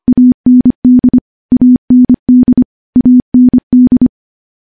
Morse code, heard as a keyed tone, about 25 wpm, 250 Hz. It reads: AND AND AND
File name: morse / morse-training-25WPM